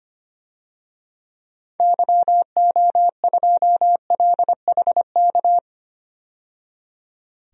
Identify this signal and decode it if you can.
Morse code, heard as a keyed tone, about 25 wpm, 700 Hz. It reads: YO2LHK
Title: DXCC-100 Call Signs - 6 Characters - 25wpm